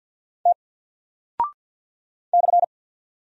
Morse code, heard as a keyed tone, about 48 wpm, 700 Hz. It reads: T /
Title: Single Character (Letter:Number:Punctuation:Prosign) Alt 50wpm